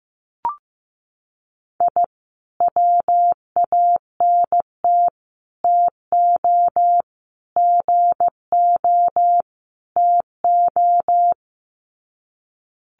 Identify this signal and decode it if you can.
Morse code, heard as a keyed tone, about 15 wpm, 700 Hz. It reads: I WANT TO GO TO
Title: Sets of 5 Words Encore 15wpm